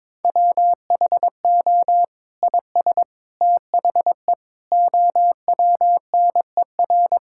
Morse code, heard as a keyed tone, about 22 wpm, 700 Hz. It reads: WHO IS THE OWNER
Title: Sentences from Top 1600 Words 22wpm